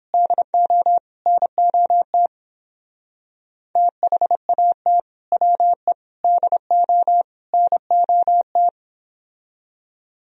Morse code, heard as a keyed tone, about 30 wpm, 700 Hz. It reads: DO NOT THAT WE DO NOT